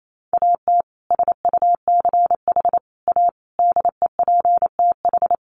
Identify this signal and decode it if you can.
Morse code, heard as a keyed tone, about 28 wpm, 700 Hz. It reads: AT SUCH A DEPTH
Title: Sentences from Top 1200 Words 28wpm